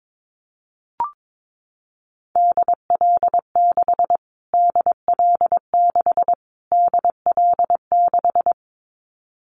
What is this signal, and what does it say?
Morse code, heard as a keyed tone, about 22 wpm, 700 Hz. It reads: DL6 DL6 DL6
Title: Callsign Prefixes for Top 25 DXCC - Repeated 3x - 22wpm